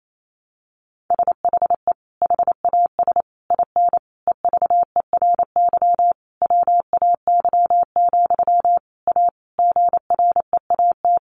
Morse code, heard as a keyed tone, about 28 wpm, 700 Hz. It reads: SHE HAS IN EVERY WAY, A GREAT
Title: Sentences from Top 1300 Words 28wpm